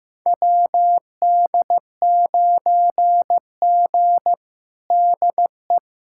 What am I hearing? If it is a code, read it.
Morse code, heard as a keyed tone, about 16 wpm, 700 Hz. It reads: WD9G DE
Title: USA Callsign DE Callsign - 4 Characters - 15wpm